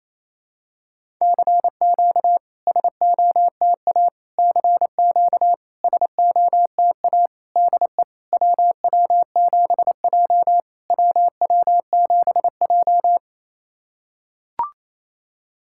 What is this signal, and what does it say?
Morse code, heard as a keyed tone, about 28 wpm, 690 Hz. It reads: CQ SOTA CQ SOTA DE WW7J WW7J
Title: SOTA QSO Practice 28wpm